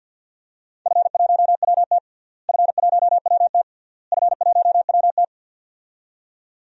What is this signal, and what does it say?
Morse code, heard as a keyed tone, about 49 wpm, 700 Hz. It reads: F1WT F1WT F1WT
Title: DXCC-100 Call Signs - 4 Characters - Repeated 3x - 50wpm